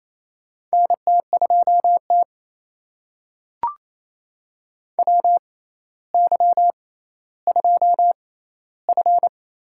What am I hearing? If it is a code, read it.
Morse code, heard as a keyed tone, about 28 wpm, spaced as for 14 wpm, 700 Hz. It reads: NT2T WY2F
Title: USA Call Signs - 4 Characters - 6x Character Spacing - 28wpm